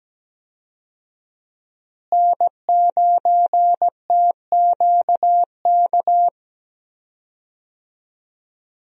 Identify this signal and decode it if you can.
Morse code, heard as a keyed tone, about 17 wpm, 700 Hz. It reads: N9TQK
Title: USA Call Signs - 5 Characters - Repeated 3x - 17wpm